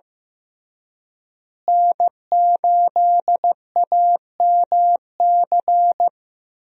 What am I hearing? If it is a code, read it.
Morse code, heard as a keyed tone, about 16 wpm, 700 Hz. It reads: N8AMC
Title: USA Call Signs - 5 Characters - Repeated 3x - 15wpm